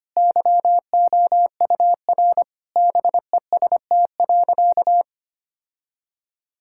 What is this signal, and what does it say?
Morse code, heard as a keyed tone, about 25 wpm, 700 Hz. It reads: YOUR BEST.
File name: practice / 25wpm / Sentences with 2-Word Phrases - 25wpm